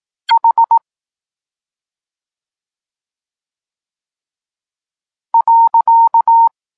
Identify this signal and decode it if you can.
Morse code, heard as a keyed tone, about 18 wpm, 920 Hz. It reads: H .
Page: Tekst 32 – Doskonalenie odbioru poznanych liter